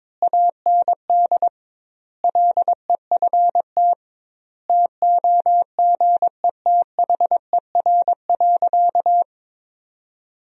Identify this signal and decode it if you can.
Morse code, heard as a keyed tone, about 22 wpm, 700 Hz. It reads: AND LEFT TOGETHER.
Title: 2x Word-Spacing - Sentences from Top 400 Words - 22wpm